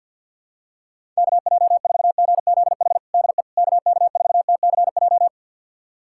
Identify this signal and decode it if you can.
Morse code, heard as a keyed tone, about 52 wpm, 700 Hz. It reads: KJ4GZH DE KK4TCJ
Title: USA Callsign DE Callsign - 6 Characters - 50wpm